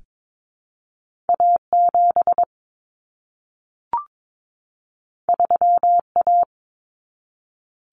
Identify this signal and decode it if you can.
Morse code, heard as a keyed tone, about 22 wpm, 700 Hz. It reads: A7 3A
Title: Common Cut Numbers - 2-Digit 22wpm